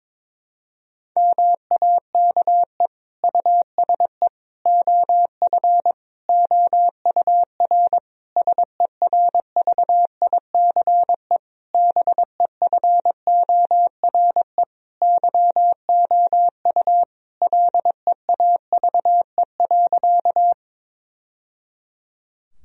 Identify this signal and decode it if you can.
Morse code, heard as a keyed tone, about 21 wpm, 700 Hz. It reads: MAKE USE OF OUR SERVICE BEFORE YOU LEAVE.
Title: Sentences from Top 1000 Words - Review Part 4 - 22wpm